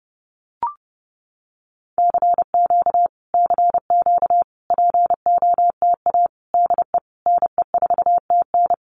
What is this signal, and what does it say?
Morse code, heard as a keyed tone, about 30 wpm, 700 Hz. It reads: CQ CQ POTA DE NE4TN
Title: Common QSO Phrases Encore 30wpm